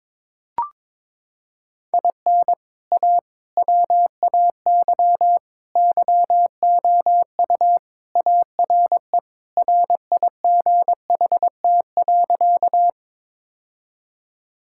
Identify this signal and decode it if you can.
Morse code, heard as a keyed tone, about 22 wpm, 700 Hz. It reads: IN A WAY YOU ARE RIGHT.
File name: practice / 22wpm / Sentences with 3-Word Phrases - 22wpm